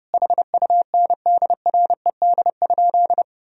Morse code, heard as a keyed tone, about 30 wpm, 700 Hz. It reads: HUNDRED?
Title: Sentences from Top 1000 Words - Review Part 2 - 30wpm